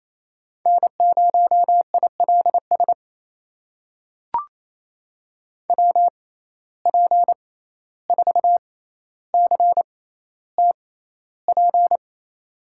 Morse code, heard as a keyed tone, about 28 wpm, spaced as for 14 wpm, 700 Hz. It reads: N0ILS WP4CTP
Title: USA Call Signs - 6x Character Spacing - 28wpm